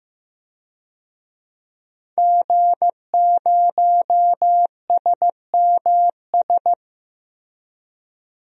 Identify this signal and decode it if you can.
Morse code, heard as a keyed tone, about 15 wpm, 700 Hz. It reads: G0SMS
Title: DXCC-100 Call Signs - 5 Characters - Repeated 2x - 15wpm